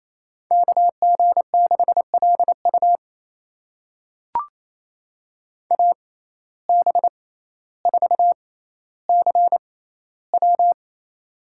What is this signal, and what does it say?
Morse code, heard as a keyed tone, about 27 wpm, spaced as for 14 wpm, 700 Hz. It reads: KG6LU AB4CW
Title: USA Call Signs - 5 Characters - 6x Character Spacing - 28wpm